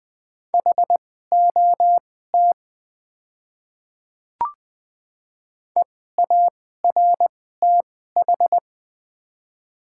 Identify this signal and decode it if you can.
Morse code, heard as a keyed tone, about 20 wpm, spaced as for 14 wpm, 700 Hz. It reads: HOT EARTH
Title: 2x Character Spacing - Top 200 Words - 20wpm